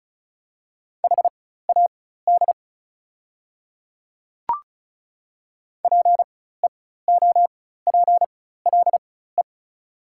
Morse code, heard as a keyed tone, about 34 wpm, spaced as for 16 wpm, 690 Hz. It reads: HAD PEOPLE
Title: 4x Character Spacing - Top 200 Words - 35wpm